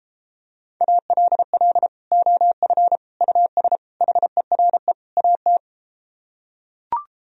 Morse code, heard as a keyed tone, about 33 wpm, 700 Hz. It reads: ALL OF US HERE AT